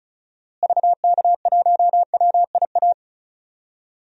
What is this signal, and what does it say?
Morse code, heard as a keyed tone, about 35 wpm, 700 Hz. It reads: VK1WIA
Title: DXCC-100 Call Signs - 6 Characters - Repeated 3x - 35wpm